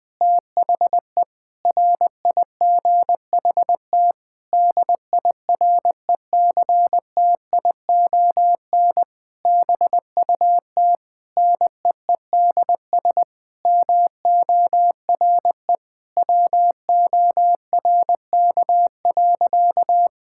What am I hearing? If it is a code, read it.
Morse code, heard as a keyed tone, about 20 wpm, 700 Hz. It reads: THE RIGHT DIRECTION BUT NEEDS MORE WORK.
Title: Sentences with 8-Word Phrases - 20wpm